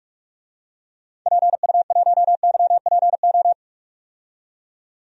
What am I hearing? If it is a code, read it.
Morse code, heard as a keyed tone, about 43 wpm, 700 Hz. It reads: PU1YPO
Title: DXCC-100 Call Signs - 6 Characters - 45wpm